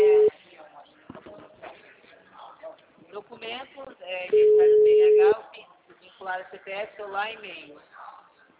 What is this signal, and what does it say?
Ringback tone (Europe, 425 Hz), with speech.
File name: out- -2006- - - .70041